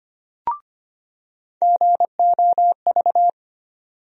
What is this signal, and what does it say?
Morse code, heard as a keyed tone, about 25 wpm, 700 Hz. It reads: GOV
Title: USA Call Sign Postfixes - 3 Letters 25wpm